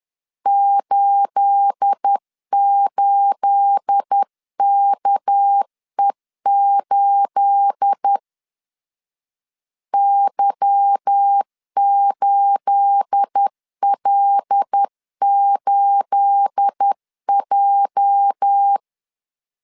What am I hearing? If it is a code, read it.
Morse code, heard as a keyed tone, about 10 wpm, 800 Hz. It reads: 88KE8 Y8L8J
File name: LEZIONI / 9 28 10wpm 2842